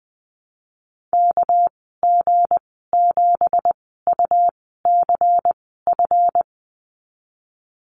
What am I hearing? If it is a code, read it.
Morse code, heard as a keyed tone, about 20 wpm, spaced as for 15 wpm, 700 Hz. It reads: KG7UCF
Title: 219 Lesson 35g - Number 0 - Call Signs 20wpm